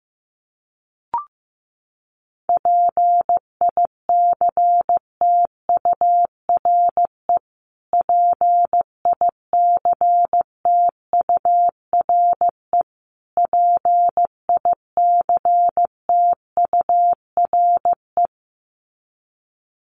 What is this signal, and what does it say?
Morse code, heard as a keyed tone, about 14 wpm, 700 Hz. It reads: PICTURE PICTURE PICTURE
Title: Top 200 Words Repeated 15wpm